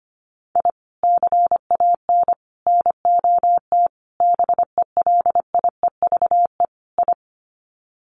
Morse code, heard as a keyed tone, about 25 wpm, 700 Hz. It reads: I CAN NOT BELIEVE I
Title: Sets of 5 Words Encore2 25wpm